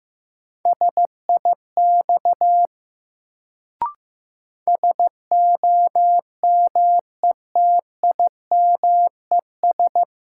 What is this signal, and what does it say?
Morse code, heard as a keyed tone, about 15 wpm, 700 Hz. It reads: SIX SOMETIMES
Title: Sentences from Top 600 Words - New Word Drill - 15wpm